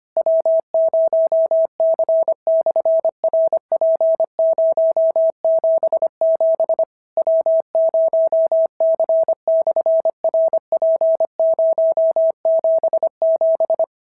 Morse code, heard as a keyed tone, about 23 wpm, 650 Hz. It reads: W0C/RP077 W0C/RP077
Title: SOTA QSO Practice 25wpm